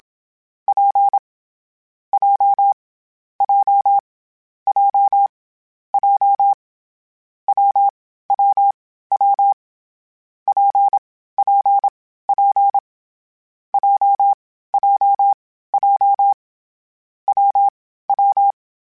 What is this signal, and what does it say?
Morse code, heard as a keyed tone, about 27 wpm, 800 Hz. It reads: P J J J J W W W P P P J J J W W
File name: Difficult Combos / Difficult Combos-mp3s / WPJ WPJ1 @25